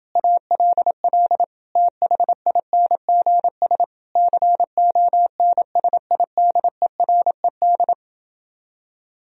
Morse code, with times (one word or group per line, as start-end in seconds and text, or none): 0.2-1.4 ALL
1.8-3.8 THINGS
4.2-7.9 CONSIDERED